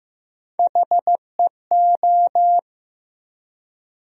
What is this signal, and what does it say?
Morse code, heard as a keyed tone, about 15 wpm, 700 Hz. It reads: HEO
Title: USA Call Sign Postfixes - 3 Letters 15wpm